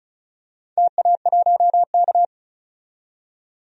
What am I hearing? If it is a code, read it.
Morse code, heard as a keyed tone, about 34 wpm, 700 Hz. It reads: TA1K
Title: DXCC-100 Call Signs - 4 Characters - 35wpm